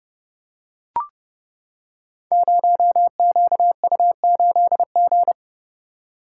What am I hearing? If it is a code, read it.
Morse code, heard as a keyed tone, about 30 wpm, 700 Hz. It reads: 0QU8G
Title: Code Groups - 5 Letter:Numbers - 30wpm